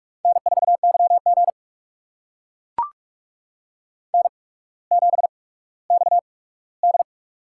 Morse code, heard as a keyed tone, about 44 wpm, spaced as for 19 wpm, 690 Hz. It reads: N4YG N7XD
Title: USA Call Signs - 4 Characters - 8x Character Spacing - 45wpm